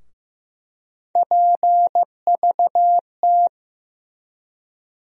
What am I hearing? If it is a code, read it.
Morse code, heard as a keyed tone, about 15 wpm, 700 Hz. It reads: PVT
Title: USA Call Sign Postfixes - 3 Letters 15wpm